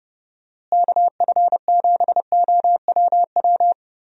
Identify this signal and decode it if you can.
Morse code, heard as a keyed tone, about 30 wpm, 700 Hz. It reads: KF7OWW
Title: USA Call Signs - 6 Characters - Repeated 3x - 30wpm